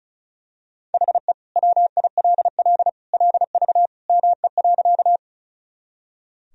Morse code, heard as a keyed tone, about 36 wpm, 700 Hz. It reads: HE WILL LV ME.
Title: Sentences from Top 400 Words and Abbreviations 35wpm